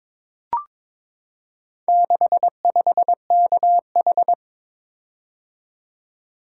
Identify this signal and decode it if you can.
Morse code, heard as a keyed tone, about 22 wpm, 700 Hz. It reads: 65KH
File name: practice / 22wpm / Four Letter-Number Combinations - 22wpm